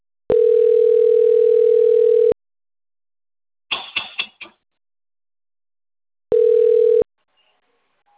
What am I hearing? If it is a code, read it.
Ringback tone (North America), 440 + 480 Hz.